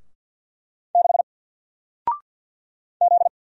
Morse code, heard as a keyed tone, about 49 wpm, 700 Hz. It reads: 6 Z
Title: Single Letter-Number Alt 50wpm